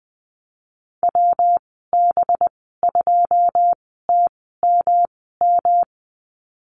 Morse code, heard as a keyed tone, about 19 wpm, spaced as for 14 wpm, 700 Hz. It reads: WB2TMM
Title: 200 Lesson 32g - Letter k - Call Signs 20wpm